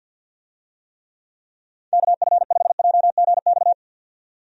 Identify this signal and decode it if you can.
Morse code, heard as a keyed tone, about 50 wpm, 700 Hz. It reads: KF5JGX
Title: USA Call Signs - 6 Characters - 50wpm